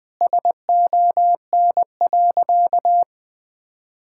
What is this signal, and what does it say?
Morse code, heard as a keyed tone, about 20 wpm, 700 Hz. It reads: SON.
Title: Sentences with 9-Word Phrases - Spoken Beginning - 20wpm